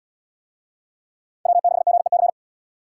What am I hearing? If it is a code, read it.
Morse code, heard as a keyed tone, about 78 wpm, 690 Hz. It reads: W6GEC